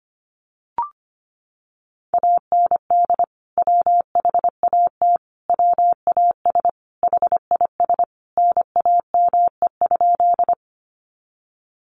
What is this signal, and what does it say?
Morse code, heard as a keyed tone, about 25 wpm, 700 Hz. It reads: AND WHAT WAS HIS NAME?
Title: Sentences from Top 200 Words 25wpm